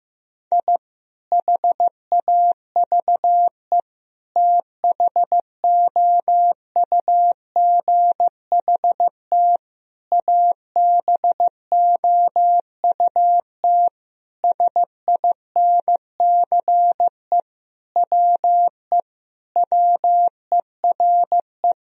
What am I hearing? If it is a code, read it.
Morse code, heard as a keyed tone, about 15 wpm, 700 Hz. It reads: I HAVE THOUGHT ABOUT SINCE WE WERE
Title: Sentences from Top 1000 Words - Review Part 1 - 15wpm